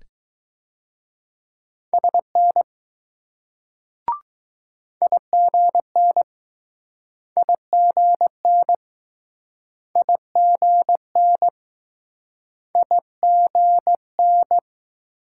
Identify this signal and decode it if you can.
Morse code, heard as a keyed tone, about 20 wpm, 700 Hz. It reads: SN IGN IGN IGN IGN